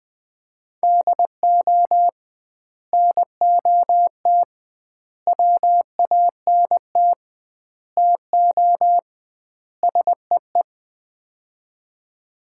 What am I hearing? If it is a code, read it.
Morse code, heard as a keyed tone, about 20 wpm, 700 Hz. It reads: DO NOT WANT TO SEE
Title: 2x Word-Spacing - Sets of 5 Words - 20wpm